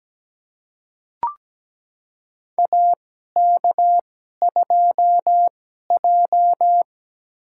Morse code, heard as a keyed tone, about 17 wpm, spaced as for 12 wpm, 700 Hz. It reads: AK2J
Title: USA Call Signs - 4 Characters - 2x Character Spacing - 17wpm